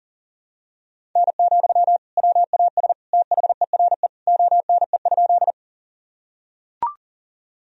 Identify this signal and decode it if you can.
Morse code, heard as a keyed tone, about 40 wpm, 700 Hz. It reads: N, WAS THERE ONE?
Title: Sentences from Top 100 Words and Abbreviations 40wpm